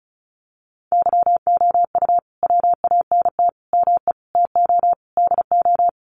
Morse code, heard as a keyed tone, about 34 wpm, 700 Hz. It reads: YOU WANT ME TO DO
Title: Sets of 5 Words Encore2 35wpm